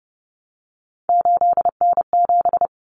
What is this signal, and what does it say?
Morse code, heard as a keyed tone, about 30 wpm, 700 Hz. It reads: 8N7
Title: Common Cut Numbers - 3-Digit 30wpm